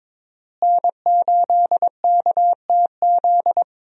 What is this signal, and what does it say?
Morse code, heard as a keyed tone, about 22 wpm, 700 Hz. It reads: N8KTZ
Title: USA Call Signs - 5 Characters - Repeated 2x - 22wpm